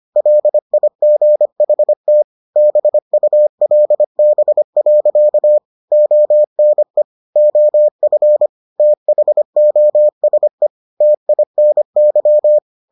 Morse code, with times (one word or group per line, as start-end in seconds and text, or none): 0.2-2.2 LIGHT
2.6-5.6 BULB.
5.9-7.0 ONE
7.4-8.5 OF
8.8-10.7 THOSE
11.0-12.6 TINY